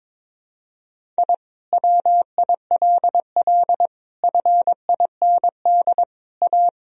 Morse code, heard as a keyed tone, about 22 wpm, 700 Hz. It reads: I WILL FIND A
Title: Sentences from Top 1000 Words - Review Part 1 - 22wpm